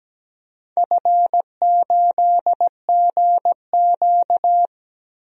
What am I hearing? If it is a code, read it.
Morse code, heard as a keyed tone, about 17 wpm, 700 Hz. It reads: F8GQ
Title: DXCC-100 Call Signs - 4 Characters - 17wpm